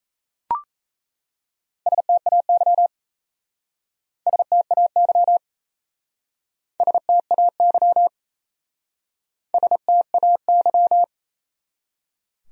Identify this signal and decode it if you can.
Morse code, heard as a keyed tone, about 37 wpm, 700 Hz. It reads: STAY STAY STAY STAY